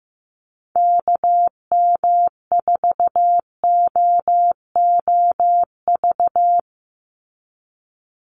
Morse code, heard as a keyed tone, about 15 wpm, 700 Hz. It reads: KM4OOV
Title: USA Call Signs 15wpm